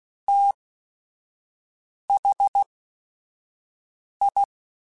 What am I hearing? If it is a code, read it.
Morse code, heard as a keyed tone, about 16 wpm, spaced as for 5 wpm, 800 Hz. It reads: THI
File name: Technical Subjects / Technical Subjects-mp3s / XL @5